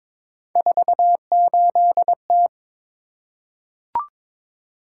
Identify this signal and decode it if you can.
Morse code, heard as a keyed tone, about 22 wpm, 700 Hz. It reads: 48T
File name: practice / 22wpm / Three Letter-Number Combinations - 22wpm